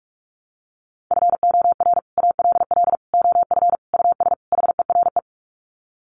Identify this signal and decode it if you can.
Morse code, heard as a keyed tone, about 44 wpm, 700 Hz. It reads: FOR ALL OF US HERE
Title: Sets of 5 Words 45wpm